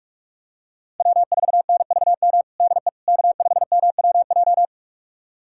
Morse code, heard as a keyed tone, about 46 wpm, 700 Hz. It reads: W4NVM DE K5MWJ
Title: USA Callsign DE Callsign - 5 Characters - 45wpm